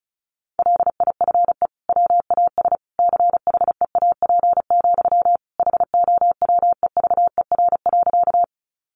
Morse code, heard as a keyed tone, about 34 wpm, 700 Hz. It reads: LIFE WAS CHEAP, HOWEVER.
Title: Sentences from Top 1200 Words 35wpm